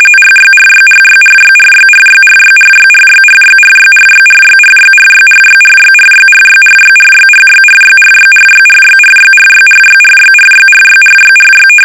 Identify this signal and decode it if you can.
RTTY, about 45.45 baud. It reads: RYRYRYRYRYRYRYRYRYRYRYRYRYRYRYRYRYRYRYRYRYRYRYRYRYRYRYRYRYRYRYRYRYRY
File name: rtty / RTTY Test 850